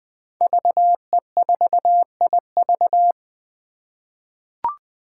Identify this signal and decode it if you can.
Morse code, heard as a keyed tone, about 20 wpm, 700 Hz. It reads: VE4IV
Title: Five Challenging Letter-Number Combinations 20wpm